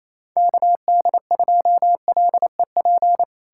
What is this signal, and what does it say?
Morse code, heard as a keyed tone, about 37 wpm, 700 Hz. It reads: KD2LEP